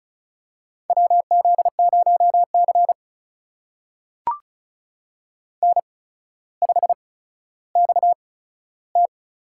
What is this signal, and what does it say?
Morse code, heard as a keyed tone, about 36 wpm, spaced as for 15 wpm, 700 Hz. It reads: WZ0C N5XT
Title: USA Call Signs - 4 Characters - 8x Character Spacing - 35wpm